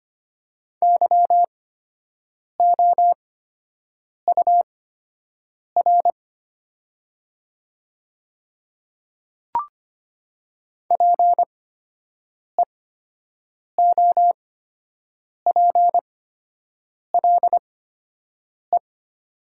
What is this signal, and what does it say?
Morse code, heard as a keyed tone, about 25 wpm, spaced as for 7 wpm, 700 Hz. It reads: YOUR PEOPLE
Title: 8x Character Spacing - Top 100 Words - 25wpm